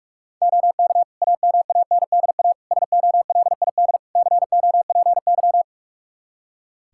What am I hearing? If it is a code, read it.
Morse code, heard as a keyed tone, about 46 wpm, 700 Hz. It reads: OK AMANDA SOLID COPY
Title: K1USN SST QSO Practice 45wpm